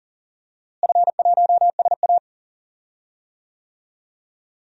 Morse code, heard as a keyed tone, about 40 wpm, 690 Hz. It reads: F1SA
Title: DXCC-100 Call Signs - 4 Characters - Repeated 2x - 40wpm